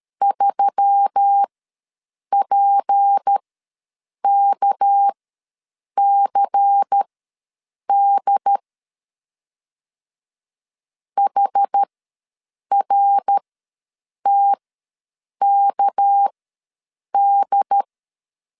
Morse code, heard as a keyed tone, about 13 wpm, spaced as for 7 wpm, 800 Hz. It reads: 3PKCD HRTKD
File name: LEZIONI / 6 PKX0 / 6 PK ws 12wpm 3116